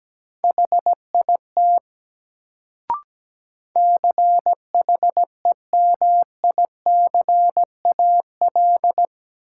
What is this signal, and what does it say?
Morse code, heard as a keyed tone, about 17 wpm, 700 Hz. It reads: HIT CHEMICAL
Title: Sentences from Top 1200 Words - New Word Drill - 17wpm